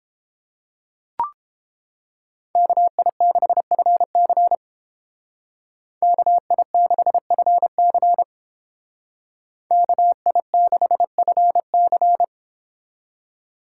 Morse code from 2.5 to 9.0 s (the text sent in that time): KI6FC KI6FC